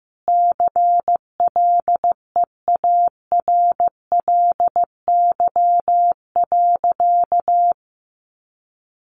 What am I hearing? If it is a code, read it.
Morse code, heard as a keyed tone, about 15 wpm, 700 Hz. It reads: CLEARLY.
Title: Sentences from Top 1300 Words 15wpm